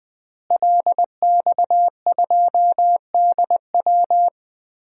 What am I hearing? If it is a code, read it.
Morse code, heard as a keyed tone, about 20 wpm, 700 Hz. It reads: LX2DW
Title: DXCC-100 Call Signs - 5 Characters - Repeated 2x - 20wpm